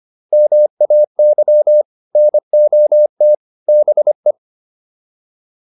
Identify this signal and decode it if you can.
Morse code, heard as a keyed tone, about 25 wpm, 600 Hz. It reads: MAY NOT BE
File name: WD205 25